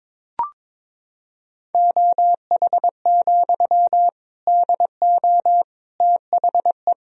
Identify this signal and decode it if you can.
Morse code, heard as a keyed tone, about 22 wpm, 700 Hz. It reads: OH, DO THE
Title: Sentences from Top 500 Words and Abbreviations 22wpm